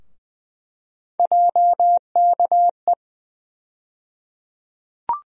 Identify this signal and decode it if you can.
Morse code, heard as a keyed tone, about 20 wpm, 700 Hz. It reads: JKE
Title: USA Call Sign Postfixes - 3 Letters 20wpm